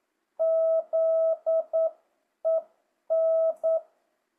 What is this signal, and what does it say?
Morse code, heard as a keyed tone, about 9 wpm, 650 Hz. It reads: ZEN
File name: jh8cbh / 2025cw